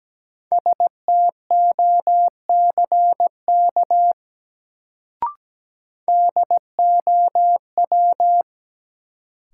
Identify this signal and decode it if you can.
Morse code, heard as a keyed tone, about 17 wpm, 700 Hz. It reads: STOCK DOW
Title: Sets of 5 Words - New Word Drill - 17wpm